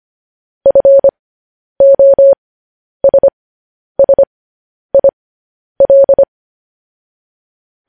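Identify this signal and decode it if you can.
Morse code, heard as a keyed tone, about 25 wpm, spaced as for 10 wpm, 550 Hz. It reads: FOSSIL
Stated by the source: Morse code news headlines transmission.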